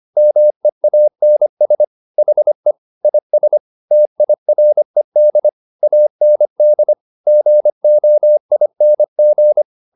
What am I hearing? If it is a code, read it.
Morse code, heard as a keyed tone, about 25 wpm, 600 Hz. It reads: MEANS HE IS TIRED AND GOING